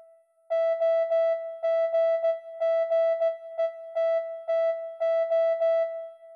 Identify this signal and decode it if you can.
Morse code, heard as a keyed tone, about 16 wpm, 660 Hz. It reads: OGGETTO